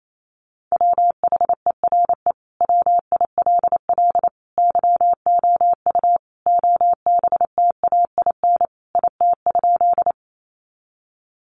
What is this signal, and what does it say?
Morse code, heard as a keyed tone, about 28 wpm, 700 Hz. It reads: WHERE WILL YOU OBTAIN IT?
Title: Sentences from Top 1300 Words 28wpm